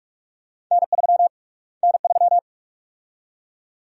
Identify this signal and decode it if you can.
Morse code, heard as a keyed tone, about 46 wpm, 700 Hz. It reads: N3 N3
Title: USA Call Sign Prefixes - Repeated 2x - 45wpm